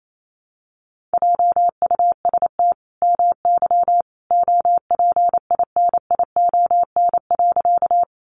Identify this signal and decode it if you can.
Morse code, heard as a keyed tone, about 28 wpm, 700 Hz. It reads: JUST MY OPINION.
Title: Sentences from Top 1200 Words 28wpm